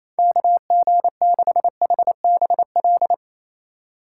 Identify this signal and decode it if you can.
Morse code, heard as a keyed tone, about 28 wpm, 700 Hz. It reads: KG6HBL
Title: USA Call Signs - 4x Character Spacing - 28wpm